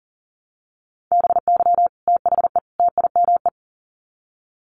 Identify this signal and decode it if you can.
Morse code, heard as a keyed tone, about 40 wpm, 700 Hz. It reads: BY THE TIME
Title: Sets of 3 Words 40wpm